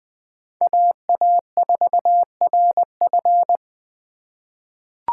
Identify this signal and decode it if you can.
Morse code, heard as a keyed tone, about 20 wpm, 700 Hz. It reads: AA4RF
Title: 2023 POTA Activator Call Signs - 20wpm